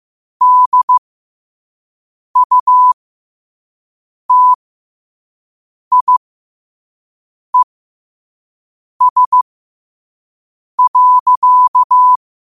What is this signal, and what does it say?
Morse code, heard as a keyed tone, about 15 wpm, spaced as for 5 wpm, 1000 Hz. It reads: DUTIES.
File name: qotdfiles / 5wpm